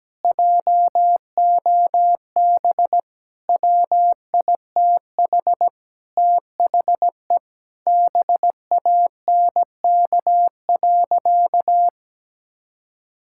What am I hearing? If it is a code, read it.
Morse code, heard as a keyed tone, about 17 wpm, 700 Hz. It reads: JOB WITH THE BANK.
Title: Sentences from Top 1000 Words - Review Part 6 - 17wpm